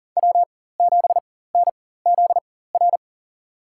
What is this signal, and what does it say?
Morse code, heard as a keyed tone, about 40 wpm, spaced as for 19 wpm, 700 Hz. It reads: W7NZR
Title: USA Call Signs - 5 Characters - 4x Character Spacing - 40wpm